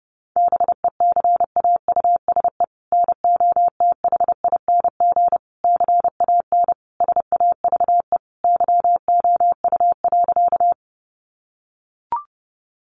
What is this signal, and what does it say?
Morse code, heard as a keyed tone, about 30 wpm, 700 Hz. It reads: BECAUSE NOTHING CAN SAVE YOU.
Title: Sentences from Top 1100 Words 30wpm